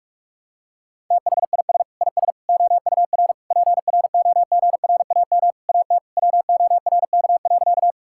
Morse code, heard as a keyed tone, about 41 wpm, 700 Hz. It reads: THIS IS OUR PROGRAM AT WORK.
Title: Sentences from Top 1000 Words - Review Part 5 - 45wpm